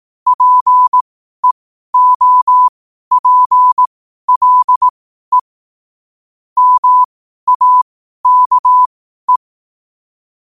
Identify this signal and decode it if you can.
Morse code, heard as a keyed tone, about 18 wpm, spaced as for 12 wpm, 1000 Hz. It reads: PEOPLE MAKE
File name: qotdfiles / 12wpm